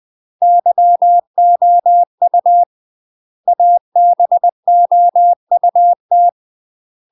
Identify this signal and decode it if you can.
Morse code, heard as a keyed tone, about 20 wpm, 700 Hz. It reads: YOU ABOUT
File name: UK MASTER QSOs TRAINING / 20WPM / UK MASTER QSOs 0035